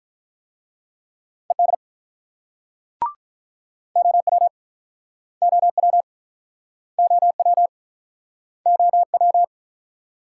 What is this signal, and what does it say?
Morse code, heard as a keyed tone, about 46 wpm, 700 Hz. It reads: ED OW OW OW OW